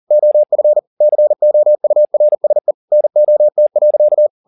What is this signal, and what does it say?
Morse code, heard as a keyed tone, about 40 wpm, 600 Hz. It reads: OF COURSE NOT.
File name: the old man and the sea 2 40wpm